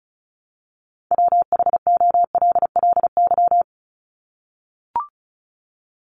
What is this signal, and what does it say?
Morse code, heard as a keyed tone, about 34 wpm, 700 Hz. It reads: WHOLLY
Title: LY Suffix - Top 100 Words with Base Words 35wpm